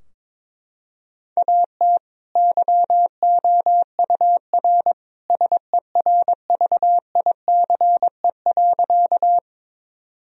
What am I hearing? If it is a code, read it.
Morse code, heard as a keyed tone, about 22 wpm, 700 Hz. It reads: AT YOUR SERVICE.
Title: Sentences from Top 1000 Words - Review Part 4 - 22wpm